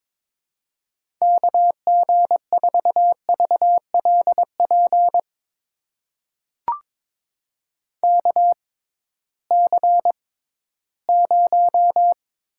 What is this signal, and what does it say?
Morse code, heard as a keyed tone, about 22 wpm, spaced as for 14 wpm, 700 Hz. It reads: KG4VLP KC0
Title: USA Call Signs - 6 Characters - 6x Character Spacing - 22wpm